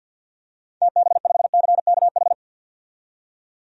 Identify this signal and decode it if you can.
Morse code, heard as a keyed tone, about 50 wpm, 700 Hz. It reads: TB5CKH
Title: DXCC-100 Call Signs - 6 Characters - 50wpm